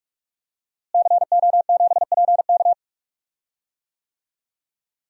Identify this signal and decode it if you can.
Morse code, heard as a keyed tone, about 47 wpm, 700 Hz. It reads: CO7PK
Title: DXCC-100 Call Signs - 5 Characters - Repeated 2x - 45wpm